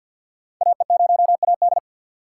Morse code, heard as a keyed tone, about 53 wpm, 700 Hz. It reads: AE0AD
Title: Code Groups - 5 Letter:Numbers - 50wpm